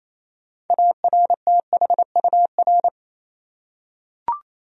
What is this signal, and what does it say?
Morse code, heard as a keyed tone, about 28 wpm, 700 Hz. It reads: ARTHUR
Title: USA Common Names - 28wpm